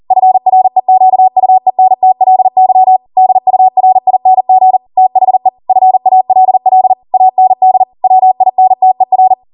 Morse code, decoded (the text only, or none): FREQUENTLY DURING THE FALL AND WINTER